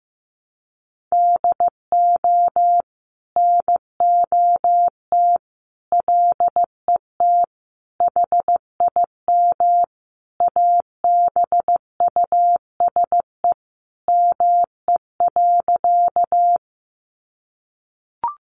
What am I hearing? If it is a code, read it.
Morse code, heard as a keyed tone, about 14 wpm, 700 Hz. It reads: DO NOT LET HIM ABUSE ME.
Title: Sentences from Top 1300 Words 15wpm